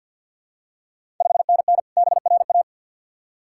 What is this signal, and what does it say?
Morse code, heard as a keyed tone, about 52 wpm, 700 Hz. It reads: 5NN BRA
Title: Ontario QSO Party Exchange 50wpm